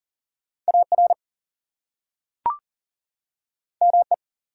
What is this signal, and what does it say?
Morse code, heard as a keyed tone, about 40 wpm, 700 Hz. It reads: AR ME
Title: US States and Canadian Provinces 40wpm